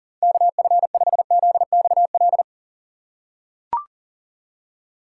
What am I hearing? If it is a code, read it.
Morse code, heard as a keyed tone, about 40 wpm, 700 Hz. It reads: KF5ZXL
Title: USA Call Signs - 6 Characters - 2x Character Spacing - 40wpm